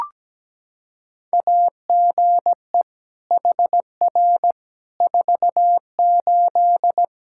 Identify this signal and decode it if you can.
Morse code, heard as a keyed tone, about 17 wpm, 700 Hz. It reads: AGE HR 48
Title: K1USN SST QSO Practice 17wpm